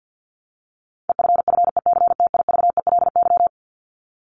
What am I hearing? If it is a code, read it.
Morse code, heard as a keyed tone, about 50 wpm, 700 Hz. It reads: EFFECTIVELY
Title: LY Suffix - Top 100 Words with Base Words 50wpm